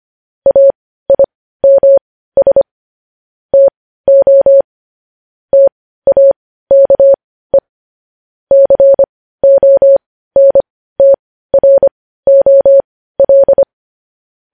Morse code, heard as a keyed tone, about 25 wpm, spaced as for 15 wpm, 550 Hz. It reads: AIMS TO TAKE CONTROL
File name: podcast / News headlines-1x-25x15